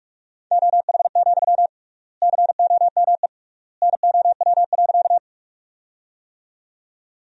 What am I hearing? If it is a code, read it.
Morse code, heard as a keyed tone, about 46 wpm, 700 Hz. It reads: OH, COME NOW.
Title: 3x Word-Spacing - Sentences from Top 500 Words - 45wpm